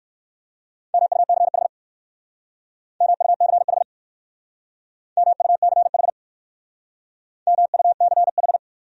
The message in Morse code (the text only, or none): MUCH MUCH MUCH MUCH